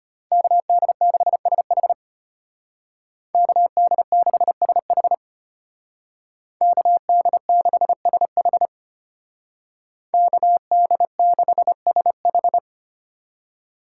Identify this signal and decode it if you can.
Morse code, heard as a keyed tone, about 30 wpm, 700 Hz. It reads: KD6SH KD6SH KD6SH KD6SH
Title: Speed Racing - USA 2x2 Call Signs 25wpm